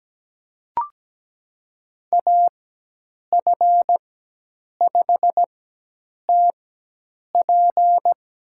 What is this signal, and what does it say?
Morse code, heard as a keyed tone, about 17 wpm, spaced as for 8 wpm, 700 Hz. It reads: AF5TP
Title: USA Call Signs - 5 Characters - 4x Character Spacing - 17wpm